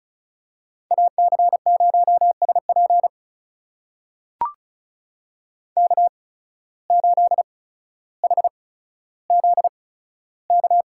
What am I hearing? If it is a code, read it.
Morse code, heard as a keyed tone, about 36 wpm, spaced as for 15 wpm, 700 Hz. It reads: AC0SP K8HZK
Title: USA Call Signs - 8x Character Spacing - 35wpm